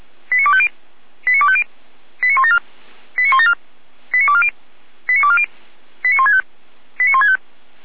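Five-tone selective call (ZVEI), heard as quick runs of five tones, tones of about 70 ms. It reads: 89250 89250 89174 89174 89240 89240 89165 89165
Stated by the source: ZVEI five-tone selective call 89337 received by SDR, logged by feuerwehr-orlamuende fire brigade server.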